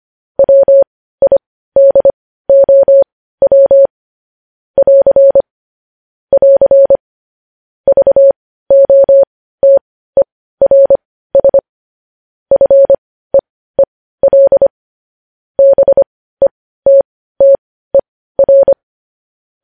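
Morse code, heard as a keyed tone, about 25 wpm, spaced as for 15 wpm, 550 Hz. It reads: WIDOW <AR> <AR> VOTERS FEEL BETTER
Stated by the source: News headlines sent in Morse code at 15 wpm.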